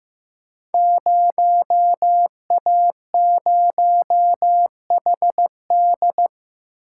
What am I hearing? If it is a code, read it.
Morse code, heard as a keyed tone, about 15 wpm, 700 Hz. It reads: 0A0HD
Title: Code Groups - 5 Letter:Numbers - 15wpm